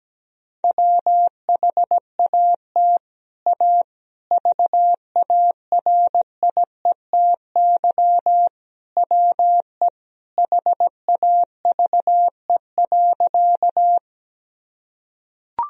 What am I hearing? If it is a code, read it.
Morse code, heard as a keyed tone, about 17 wpm, 700 Hz. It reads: WHAT A VARIETY WE HAVE.
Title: Sentences from Top 1000 Words - Review Part 4 - 17wpm